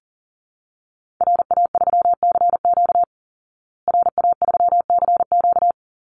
Morse code, heard as a keyed tone, about 40 wpm, 700 Hz. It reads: RA3CQ RA3CQ
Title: Call Signs Repeated 3x - 40wpm